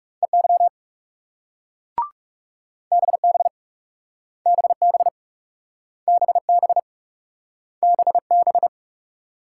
Morse code, heard as a keyed tone, about 42 wpm, 700 Hz. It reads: EY BB BB BB BB